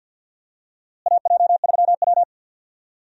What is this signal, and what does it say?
Morse code, heard as a keyed tone, about 52 wpm, 700 Hz. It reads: AJ3W
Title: USA Call Signs - 4 Characters - Repeated 2x - 50wpm